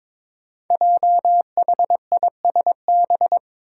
Morse code, heard as a keyed tone, about 23 wpm, 700 Hz. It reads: JHISB
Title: Five Challenging Letter-Number Combinations 22wpm